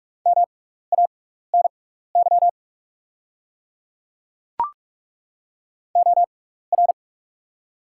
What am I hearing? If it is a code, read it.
Morse code, heard as a keyed tone, about 43 wpm, spaced as for 15 wpm, 690 Hz. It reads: MANY OR
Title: 6x Character Spacing - Top 100 Words - 45wpm